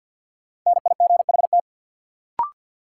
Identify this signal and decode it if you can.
Morse code, heard as a keyed tone, about 47 wpm, 700 Hz. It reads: NIGHT
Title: Sets of 4 Words - New Word Drill - 50wpm